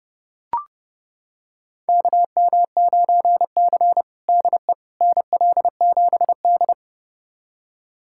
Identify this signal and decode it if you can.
Morse code, heard as a keyed tone, about 30 wpm, 700 Hz. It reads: KM9C DE NL7D
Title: USA Callsign DE Callsign - 4 Characters - 30wpm